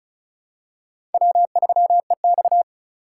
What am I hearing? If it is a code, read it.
Morse code, heard as a keyed tone, about 34 wpm, 700 Hz. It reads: W3EX
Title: USA Call Signs - 3x Character Spacing - 35wpm